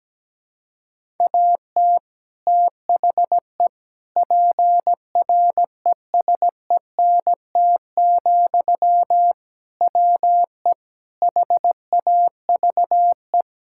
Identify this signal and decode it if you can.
Morse code, heard as a keyed tone, about 17 wpm, 700 Hz. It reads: AT THE PRESENT, WE HAVE
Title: Sentences from Top 1000 Words - Review Part 5 - 17wpm